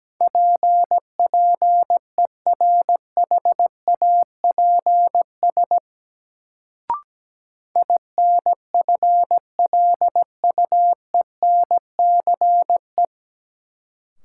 Morse code, heard as a keyed tone, about 17 wpm, 700 Hz. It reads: PPERHAPS INFLUENCE
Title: Sentences from Top 1100 Words - New Word Drill Alt - 17wpm